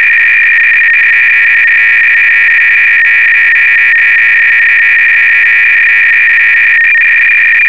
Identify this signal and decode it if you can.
300-baud modem data.
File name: Repository / ACR-Audio-Cassette-Programs / MS-8K-BASIC-V4